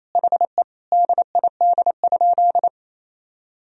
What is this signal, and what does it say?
Morse code, heard as a keyed tone, about 28 wpm, 700 Hz. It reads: HE DID?
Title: Sentences from Top 100 Words and Abbreviations 28wpm